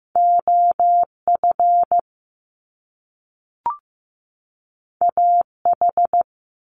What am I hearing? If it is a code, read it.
Morse code, heard as a keyed tone, about 15 wpm, 700 Hz. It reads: OF AH
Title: Top Two Letter Combinations in Words 15wpm